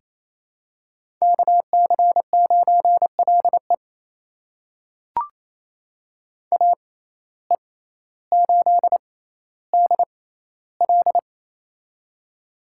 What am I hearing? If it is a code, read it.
Morse code, heard as a keyed tone, about 28 wpm, spaced as for 14 wpm, 700 Hz. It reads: KC9LE AE8DL
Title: USA Call Signs - 5 Characters - 6x Character Spacing - 28wpm